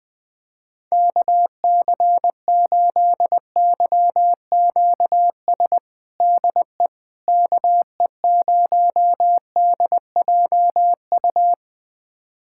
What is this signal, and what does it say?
Morse code, heard as a keyed tone, about 20 wpm, 700 Hz. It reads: KC8YQS DE KE0DJU
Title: USA Callsign DE Callsign 20wpm